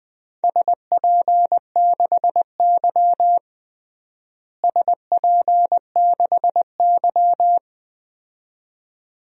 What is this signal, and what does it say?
Morse code, heard as a keyed tone, about 20 wpm, 700 Hz. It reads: SP6Y SP6Y
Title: DXCC-100 Call Signs - 4 Characters - Repeated 2x - 20wpm